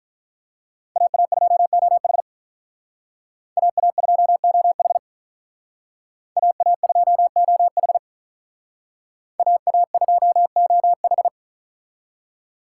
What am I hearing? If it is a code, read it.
Morse code, heard as a keyed tone, about 45 wpm, 700 Hz. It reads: AA2OH AA2OH AA2OH AA2OH